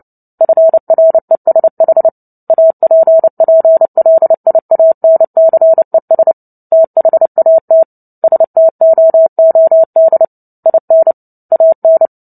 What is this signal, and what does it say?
Morse code, heard as a keyed tone, about 30 wpm, 650 Hz. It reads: FRESH APPLIANCES THAT STOOD IN AN
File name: Book Two / 03-Chapter 3 - 29 WPM